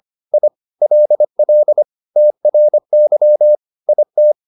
Morse code, heard as a keyed tone, about 25 wpm, 600 Hz. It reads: I LL TRY IT